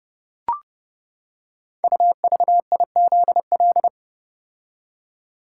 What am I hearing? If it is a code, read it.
Morse code, heard as a keyed tone, about 30 wpm, 700 Hz. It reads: UVIZL
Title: Code Groups - 5 Letters - 30wpm